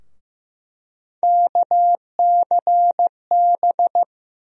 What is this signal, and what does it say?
Morse code, heard as a keyed tone, about 15 wpm, 700 Hz. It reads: KCB
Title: USA Call Sign Postfixes - 3 Letters 15wpm